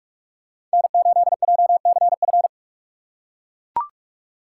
Morse code, heard as a keyed tone, about 40 wpm, 700 Hz. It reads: N8JCF